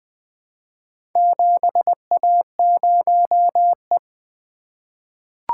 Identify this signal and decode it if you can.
Morse code, heard as a keyed tone, about 20 wpm, 700 Hz. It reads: 7A0E
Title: Four Letter-Number Combinations - 20wpm